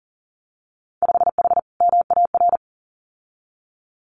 Morse code, heard as a keyed tone, about 40 wpm, 700 Hz. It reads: 5H MAR
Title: Winter Field Day Exchanges Encore 40wpm